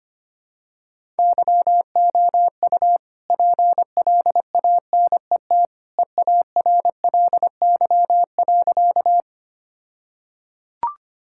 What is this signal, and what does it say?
Morse code, heard as a keyed tone, about 25 wpm, 700 Hz. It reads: YOU PLANET EARLY.
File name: practice / 25wpm / All-Age Appropriate Jokes - 25wpm